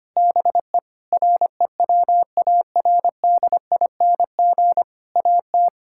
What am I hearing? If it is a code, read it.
Morse code, heard as a keyed tone, about 25 wpm, 700 Hz. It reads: BE REWARDING AT
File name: practice / 25wpm / Difficult Sentences with Hints - 25wpm